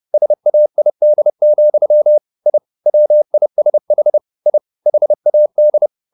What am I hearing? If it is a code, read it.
Morse code, heard as a keyed tone, about 30 wpm, 600 Hz. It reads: SAID, I WISH I HAD
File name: the old man and the sea / the old man and the sea 1 30wpm